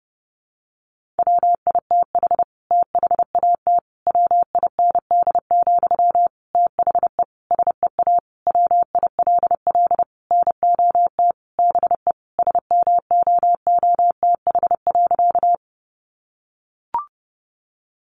Morse code, heard as a keyed tone, about 30 wpm, 700 Hz. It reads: WITH THAT WIND, THE SEA WILL NOT BE SMOOTH.
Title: Sentences from Top 1400 Words 30wpm